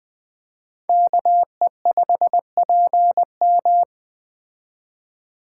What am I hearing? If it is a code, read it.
Morse code, heard as a keyed tone, about 20 wpm, 700 Hz. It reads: KE5PM
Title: Speed Racing - USA 2x2 Call Signs 17wpm